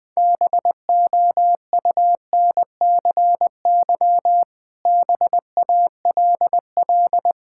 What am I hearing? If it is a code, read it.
Morse code, heard as a keyed tone, about 20 wpm, 700 Hz. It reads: BOUNCY BALL
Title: Difficult Words with Hints - 20wpm